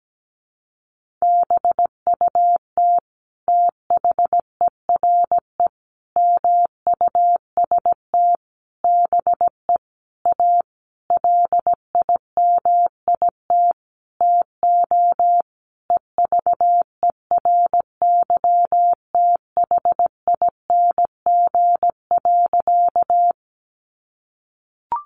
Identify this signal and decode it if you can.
Morse code, heard as a keyed tone, about 17 wpm, 700 Hz. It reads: BUT THERE MUST BE A LIMIT TO EVERYTHING.
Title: Sentences from Top 1500 Words 17wpm